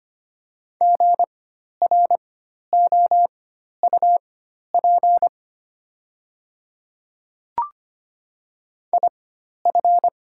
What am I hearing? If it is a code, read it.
Morse code, heard as a keyed tone, about 25 wpm, spaced as for 12 wpm, 700 Hz. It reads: GROUP IF
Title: 4x Character Spacing - Top 300 Words - 25wpm